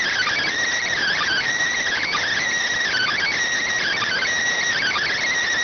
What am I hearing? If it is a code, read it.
Emergency Alert System (SAME) tones (header bursts), heard as harsh buzzing bursts. It is heard from 0.9 to 5.3 s.